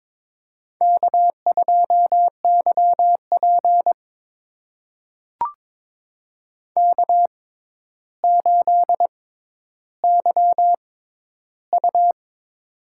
Morse code, heard as a keyed tone, about 22 wpm, spaced as for 11 wpm, 700 Hz. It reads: K2YP K8YU
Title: USA Call Signs - 4 Characters - 6x Character Spacing - 22wpm